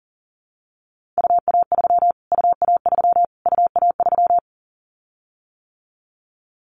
Morse code, heard as a keyed tone, about 40 wpm, 700 Hz. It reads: UA3 UA3 UA3
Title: Callsign Prefixes for Top 25 DXCC - Repeated 3x - 40wpm